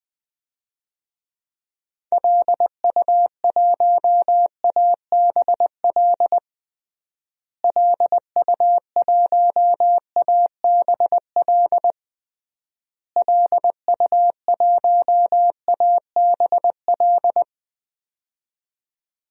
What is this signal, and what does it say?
Morse code, heard as a keyed tone, about 20 wpm, 700 Hz. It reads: LU1ABL LU1ABL LU1ABL
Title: DXCC-100 Call Signs - 6 Characters - Repeated 3x - 20wpm